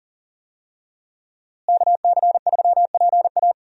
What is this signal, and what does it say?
Morse code, heard as a keyed tone, about 39 wpm, 700 Hz. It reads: KC3PA
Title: USA Call Signs - 5 Characters - Repeated 3x - 40wpm